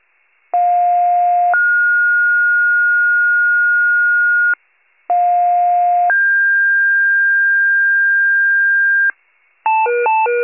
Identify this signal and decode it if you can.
Two-tone page, heard as a one-second tone, then a three-second tone, about 710 then 1430 Hz, about 710 then 1600 Hz, with noise. It is heard from 0.5 to 9.1 s.